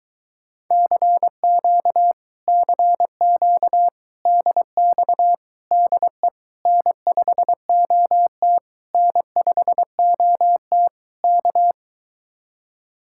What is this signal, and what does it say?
Morse code, heard as a keyed tone, about 24 wpm, 700 Hz. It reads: CQ CQ DX DE N5OT N5OT K